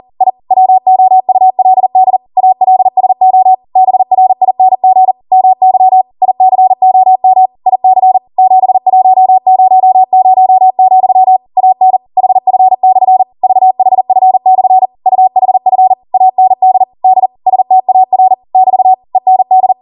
Morse code, heard as a keyed tone, about 40 wpm, 750 Hz. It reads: I WOULD ALSO BRING MY ICOM IC 7100, AN HF/ VHF/ UHF AND D STAR <BT> END